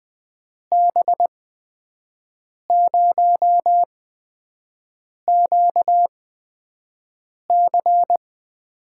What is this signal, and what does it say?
Morse code, heard as a keyed tone, about 20 wpm, spaced as for 5 wpm, 700 Hz. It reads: B0QC
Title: USA Call Signs - 6 Characters - 8x Character Spacing - 20wpm